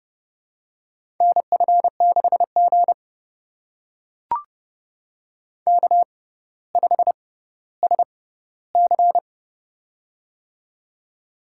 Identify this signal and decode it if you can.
Morse code, heard as a keyed tone, about 30 wpm, spaced as for 15 wpm, 690 Hz. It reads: NF6G K5SC
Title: USA Call Signs - 4 Characters - 6x Character Spacing - 30wpm